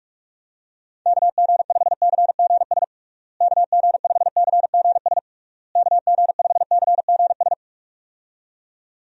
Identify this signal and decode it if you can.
Morse code, heard as a keyed tone, about 46 wpm, 700 Hz. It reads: KG5CGS KG5CGS KG5CGS
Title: USA Call Signs - 6 Characters - Repeated 3x - 45wpm